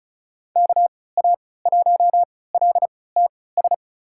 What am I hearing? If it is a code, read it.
Morse code, heard as a keyed tone, about 36 wpm, spaced as for 20 wpm, 700 Hz. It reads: KA1LTS
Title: USA Call Signs - 3x Character Spacing - 35wpm